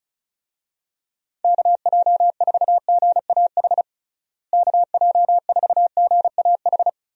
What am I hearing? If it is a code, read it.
Morse code, heard as a keyed tone, about 34 wpm, 700 Hz. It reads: KJ4GAH KJ4GAH
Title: USA Call Signs - 6 Characters - Repeated 3x - 35wpm